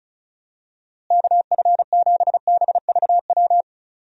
Morse code, heard as a keyed tone, about 36 wpm, 700 Hz. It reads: KF7BVW
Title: USA Call Signs - 6 Characters - Repeated 3x - 35wpm